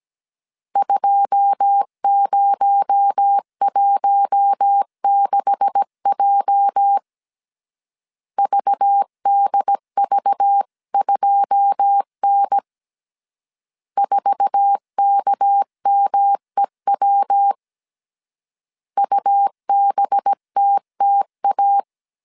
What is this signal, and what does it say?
Morse code, heard as a keyed tone, about 17 wpm, 800 Hz. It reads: 2016J VDV2N 4KMEW UBTTA
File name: CW / DET / 1 9 1500C S A 16wpm 2828